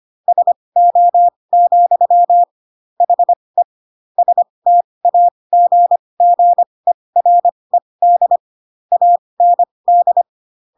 Morse code, heard as a keyed tone, about 25 wpm, spaced as for 20 wpm, 700 Hz. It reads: SO, HE STAGGERED AND
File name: Pinocchio / Pinocchio ch7 15 25 ik1hge